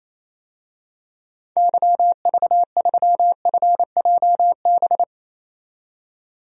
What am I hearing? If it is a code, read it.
Morse code, heard as a keyed tone, about 27 wpm, 700 Hz. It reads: YV3FJB
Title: DXCC-100 Call Signs - 6 Characters - 28wpm